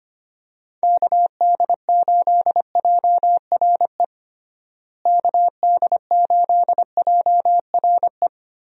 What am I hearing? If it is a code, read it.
Morse code, heard as a keyed tone, about 25 wpm, 700 Hz. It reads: KD8JRE KD8JRE
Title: USA Call Signs - 6 Characters - Repeated 3x - 25wpm